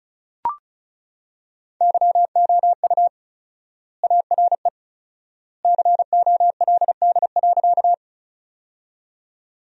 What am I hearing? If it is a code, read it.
Morse code, heard as a keyed tone, about 34 wpm, 700 Hz. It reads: YOU ARE COLD.
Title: 4x Word-Spacing - Sentences from Top 400 Words - 35wpm